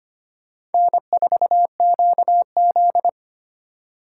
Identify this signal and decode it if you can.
Morse code, heard as a keyed tone, about 30 wpm, 700 Hz. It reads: N4QZ
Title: Speed Racing - USA 1x2 Call Signs 25wpm